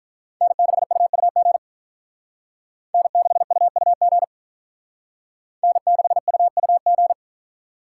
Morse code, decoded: N6UUG N6UUG N6UUG

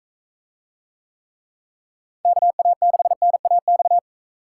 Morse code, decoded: KA6NAX

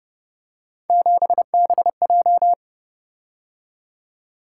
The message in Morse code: 7BJ

